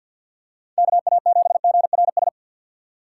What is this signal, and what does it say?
Morse code, heard as a keyed tone, about 53 wpm, 700 Hz. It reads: KA7GRS
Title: USA Call Signs - 6 Characters - 4x Character Spacing - 50wpm